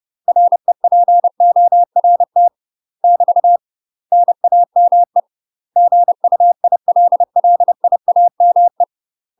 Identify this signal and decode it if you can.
Morse code, heard as a keyed tone, about 30 wpm, 700 Hz. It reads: REPORT <BT> NAME GUILLIAME
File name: UK MASTER QSOs TRAINING / 30WPM / UK MASTER QSOs 0023